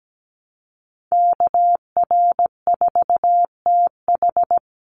Morse code, heard as a keyed tone, about 17 wpm, 700 Hz. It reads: KR4TH
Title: Call Signs Repeated 3x - 17wpm